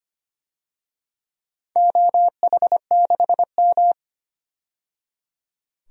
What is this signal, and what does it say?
Morse code, heard as a keyed tone, about 25 wpm, 700 Hz. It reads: OH6M
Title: DXCC-100 Call Signs - 4 Characters - 25wpm